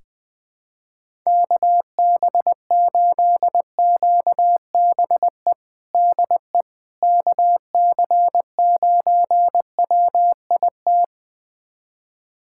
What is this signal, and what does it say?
Morse code, heard as a keyed tone, about 20 wpm, 700 Hz. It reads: KB8QBE DE KC9WIT
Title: USA Callsign DE Callsign 20wpm